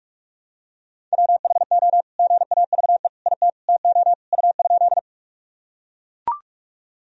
Morse code, heard as a keyed tone, about 41 wpm, 700 Hz. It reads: WHO GAVE IT TO U?
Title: Sentences from Top 500 Words and Abbreviations 45wpm